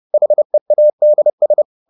Morse code, heard as a keyed tone, about 30 wpm, 600 Hz. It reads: HEADS